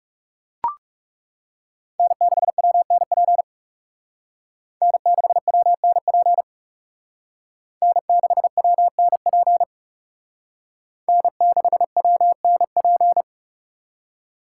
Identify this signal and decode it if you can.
Morse code, heard as a keyed tone, about 39 wpm, 700 Hz. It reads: N6WNP N6WNP N6WNP N6WNP